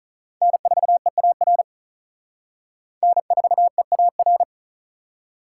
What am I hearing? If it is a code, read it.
Morse code, heard as a keyed tone, about 39 wpm, 700 Hz. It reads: N4EAR N4EAR
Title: Speed Racing - USA 1x3 Call Signs 35wpm